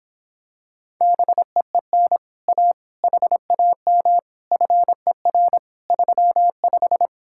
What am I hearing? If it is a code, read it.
Morse code, heard as a keyed tone, about 26 wpm, 700 Hz. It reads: BEEN A HAM FER 35